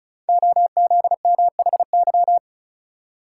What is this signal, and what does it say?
Morse code, heard as a keyed tone, about 36 wpm, 690 Hz. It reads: OZMHY
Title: Code Groups - 5 Letters - 35wpm